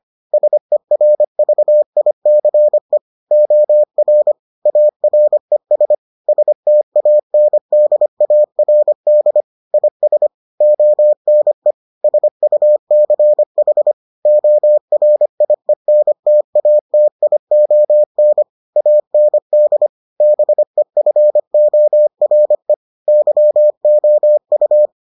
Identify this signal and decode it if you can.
Morse code, heard as a keyed tone, about 25 wpm, 600 Hz. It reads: SERVICE OR ARES STANDARD IS ONE SUCH ORIENTATION AND BEFORE YOU